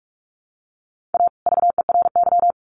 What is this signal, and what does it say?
Morse code, heard as a keyed tone, about 44 wpm, 700 Hz. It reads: A VERY
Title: Sets of 2 Words 45wpm